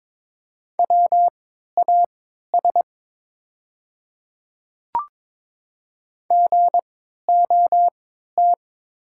Morse code, heard as a keyed tone, about 22 wpm, spaced as for 12 wpm, 700 Hz. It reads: WAS GOT
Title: 3x Character Spacing - Top 500 Words - 22wpm